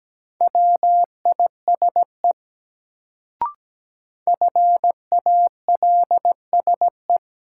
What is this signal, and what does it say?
Morse code, heard as a keyed tone, about 17 wpm, 700 Hz. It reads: WISE FALSE
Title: Sentences from Top 1600 Words - New Word Drill - 17wpm